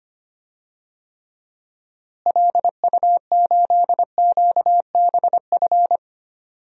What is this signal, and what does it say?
Morse code, heard as a keyed tone, about 25 wpm, 700 Hz. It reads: LU8QBF
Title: DXCC-100 Call Signs - 6 Characters - Repeated 2x - 25wpm